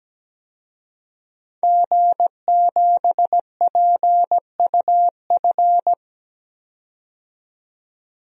Morse code, heard as a keyed tone, about 17 wpm, 700 Hz. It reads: G7PUF
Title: DXCC-100 Call Signs - 5 Characters - 17wpm